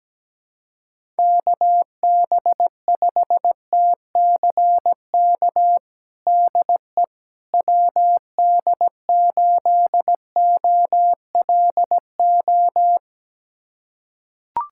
Morse code, heard as a keyed tone, about 17 wpm, 700 Hz. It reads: KB5TCK DE WD8OLO
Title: USA Callsign DE Callsign - 6 Characters - 17wpm